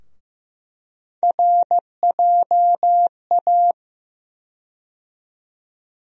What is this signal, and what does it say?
Morse code, heard as a keyed tone, about 15 wpm, 700 Hz. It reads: RJA
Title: USA Call Sign Postfixes - 3 Letters 15wpm